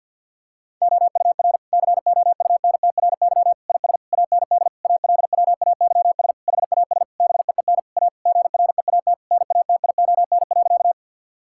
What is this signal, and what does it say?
Morse code, heard as a keyed tone, about 48 wpm, 700 Hz. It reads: OUR COUNTRY IS AND ALWAYS HAS BEEN A GREAT NATION.
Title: Sentences from Top 1000 Words - Review Part 6 - 50wpm